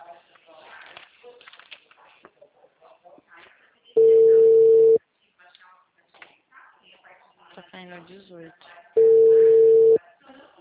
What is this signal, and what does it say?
Ringback tone (China, 450 Hz), with speech.
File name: out- -2003- - - .30372